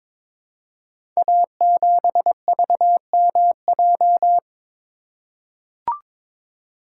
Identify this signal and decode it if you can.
Morse code, heard as a keyed tone, about 22 wpm, 690 Hz. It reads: A7VMJ
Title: Code Groups - 5 Letter:Numbers - 22wpm